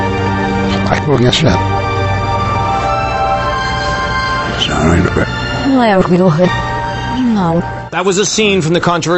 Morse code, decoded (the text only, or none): none